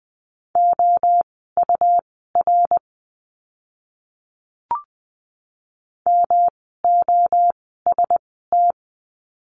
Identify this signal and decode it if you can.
Morse code, heard as a keyed tone, about 20 wpm, spaced as for 14 wpm, 700 Hz. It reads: OUR MOST
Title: 115 Lesson 19f - Letter w - Words 20wpm